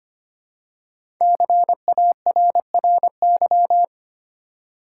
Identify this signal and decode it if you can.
Morse code, heard as a keyed tone, about 25 wpm, 700 Hz. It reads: CARRY
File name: 5 Letter Words - First 3 Letters Given - 25wpm